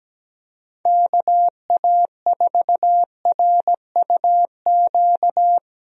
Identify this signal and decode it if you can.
Morse code, heard as a keyed tone, about 17 wpm, 700 Hz. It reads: KA4RUQ
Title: USA Call Signs - 6 Characters - Repeated 2x - 17wpm